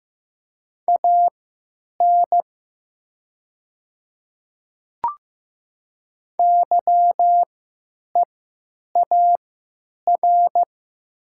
Morse code, heard as a keyed tone, about 15 wpm, spaced as for 9 wpm, 700 Hz. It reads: AN YEAR
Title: 3x Character Spacing - Top 200 Words - 15wpm